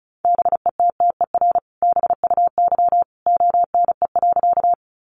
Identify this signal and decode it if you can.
Morse code, heard as a keyed tone, about 34 wpm, 700 Hz. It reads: BETTER BUY ONE.
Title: Sentences from Top 700 Words 35wpm